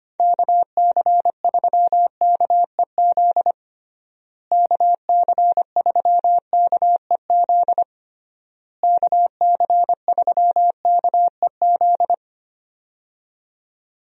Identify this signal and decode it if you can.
Morse code, heard as a keyed tone, about 25 wpm, 700 Hz. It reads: KC3KEZ KC3KEZ KC3KEZ
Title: USA Call Signs - 6 Characters - Repeated 3x - 25wpm